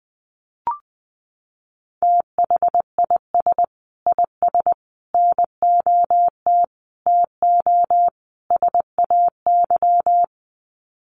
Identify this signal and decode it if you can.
Morse code, heard as a keyed tone, about 20 wpm, 700 Hz. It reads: THIS IS NOT TO SAY
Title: Sets of 5 Words Encore 20wpm